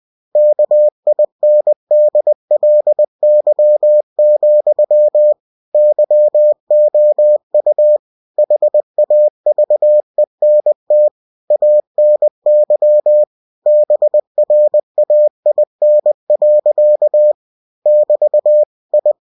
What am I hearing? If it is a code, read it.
Morse code, heard as a keyed tone, about 20 wpm, 600 Hz. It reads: KINDLY, YOU HAVENT ANY BRAIN. <BT> I
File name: cw-books / winnie-the-pooh / winnie-20wpm-page-0061